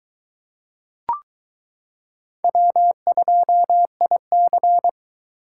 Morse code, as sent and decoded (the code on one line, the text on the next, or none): .-- ..--- .. -.-.
W2IC